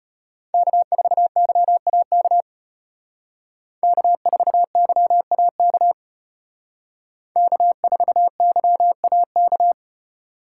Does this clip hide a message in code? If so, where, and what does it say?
Yes: Morse code from 0.5 to 9.7 s, heard as a keyed tone, about 33 wpm, 700 Hz. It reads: K4YAK K4YAK K4YAK